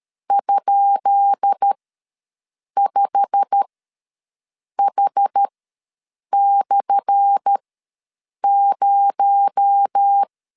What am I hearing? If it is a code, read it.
Morse code, heard as a keyed tone, about 13 wpm, spaced as for 6 wpm, 800 Hz. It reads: ?5H/0
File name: LEZIONI / 10 SIGNS / 10B INT ws 12wpm 3527